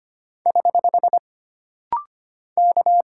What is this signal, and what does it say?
Morse code, heard as a keyed tone, about 25 wpm, 700 Hz. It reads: <HH> K
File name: Single Character (Letter:Number:Punctuation:Prosign) - Rapid-Fire - 25wpm